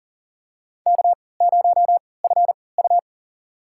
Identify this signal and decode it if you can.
Morse code, heard as a keyed tone, about 40 wpm, spaced as for 23 wpm, 700 Hz. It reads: K0FU
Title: USA Call Signs - 4 Characters - 3x Character Spacing - 40wpm